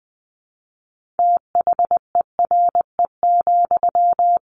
Morse code, heard as a keyed tone, about 20 wpm, 700 Hz. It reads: THERE,
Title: Sentences from Top 1200 Words 20wpm